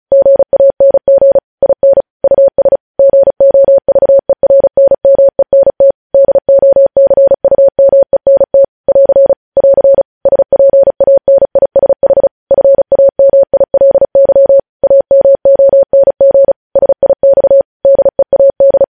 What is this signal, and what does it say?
Morse code, heard as a keyed tone, about 36 wpm, 550 Hz. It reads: GANG IN US GOVERNMENT DOCUMENT <AR> <AR> SPANISH FAMILY AMONG SIX DEAD
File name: podcast / News headlines-1x-35x35